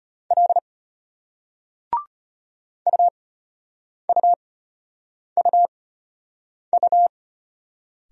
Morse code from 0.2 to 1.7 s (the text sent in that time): L